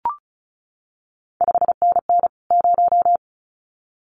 Morse code, heard as a keyed tone, about 36 wpm, 700 Hz. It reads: 5NN 0
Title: CQ WW DX Contest Exchange Encore - 35wpm